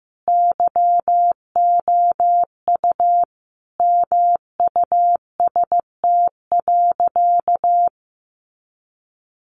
Morse code, heard as a keyed tone, about 15 wpm, 700 Hz. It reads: YOU MUST.
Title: Sentences from Top 200 Words 15wpm